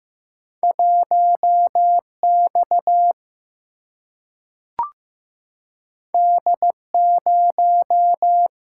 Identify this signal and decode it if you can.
Morse code, heard as a keyed tone, about 15 wpm, 700 Hz. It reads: 1 X D 0
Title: Single-Letter Single-Number 15wpm